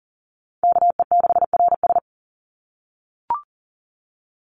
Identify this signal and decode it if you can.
Morse code, heard as a keyed tone, about 40 wpm, 700 Hz. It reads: KE6RS
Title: NA SOTA Chaser Callsigns 40wpm